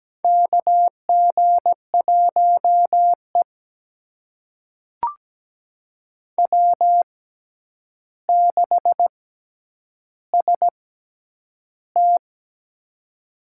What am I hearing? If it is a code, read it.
Morse code, heard as a keyed tone, about 17 wpm, spaced as for 9 wpm, 700 Hz. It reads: KG1E W6ST
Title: USA Call Signs - 6x Character Spacing - 17wpm